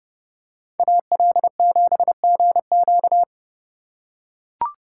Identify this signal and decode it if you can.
Morse code, heard as a keyed tone, about 29 wpm, 700 Hz. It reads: AL7GQ